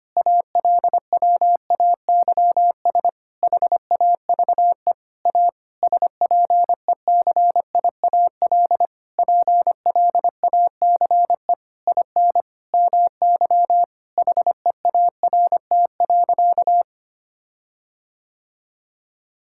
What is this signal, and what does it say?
Morse code, heard as a keyed tone, about 25 wpm, 700 Hz. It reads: ALWAYS HAVE A SPECIAL PLACE IN MY HEART.
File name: practice / 25wpm / Sentences with 9-Word Phrases - 25wpm